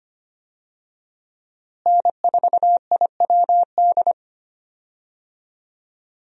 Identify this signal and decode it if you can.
Morse code, heard as a keyed tone, about 25 wpm, 700 Hz. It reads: N4IWD
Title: USA Call Signs - 5 Characters - Repeated 3x - 25wpm